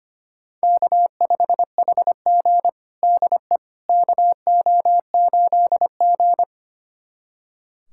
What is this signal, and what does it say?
Morse code, heard as a keyed tone, about 25 wpm, 700 Hz. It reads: K5HG DE KO8G
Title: USA Callsign DE Callsign - 4 Characters - 25wpm